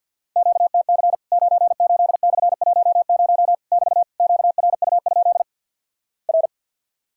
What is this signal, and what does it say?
Morse code, heard as a keyed tone, about 50 wpm, 700 Hz. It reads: QTC 98/10 <BT> QRU? R
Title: Worked All Europe DX Contest - QTC Exchange - 50wpm